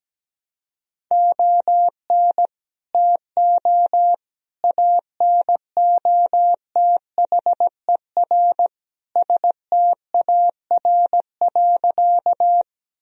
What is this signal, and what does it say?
Morse code, heard as a keyed tone, about 17 wpm, 700 Hz. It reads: ON TO ANOTHER STAR.
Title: Sentences from Top 1000 Words - Review Part 2 - 17wpm